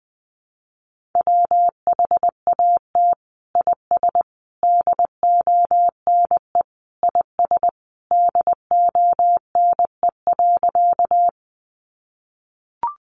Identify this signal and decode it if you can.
Morse code, heard as a keyed tone, about 20 wpm, 700 Hz. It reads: WHAT IS DONE IS DONE.
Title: Sentences from Top 500 Words 20wpm